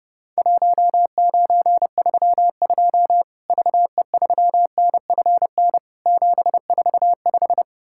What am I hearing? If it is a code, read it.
Morse code, heard as a keyed tone, about 30 wpm, 700 Hz. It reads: 1932 VE3NFN 745
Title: Worked All Europe DX Contest - QTC Exchange - 30wpm